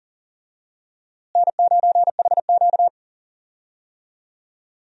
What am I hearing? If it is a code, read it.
Morse code, heard as a keyed tone, about 37 wpm, 700 Hz. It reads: N9HQ
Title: USA Call Signs - 4 Characters - Repeated 2x - 40wpm